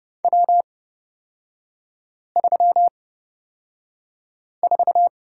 Morse code, heard as a keyed tone, about 30 wpm, spaced as for 5 wpm, 700 Hz. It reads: W34
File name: Lesson 38e - Letter q - Three Characters 30wpm